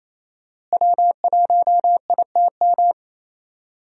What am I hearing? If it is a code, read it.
Morse code, heard as a keyed tone, about 27 wpm, 700 Hz. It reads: W1ITM
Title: USA Call Signs - 5 Characters - Repeated 2x - 28wpm